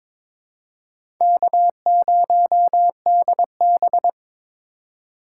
Morse code, heard as a keyed tone, about 22 wpm, 700 Hz. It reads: K0DB
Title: USA Call Signs - 4 Characters - 2x Character Spacing - 22wpm